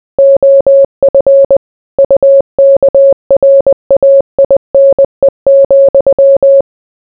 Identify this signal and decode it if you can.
Morse code, heard as a keyed tone, about 20 wpm, 550 Hz. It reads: OF UKRAINE,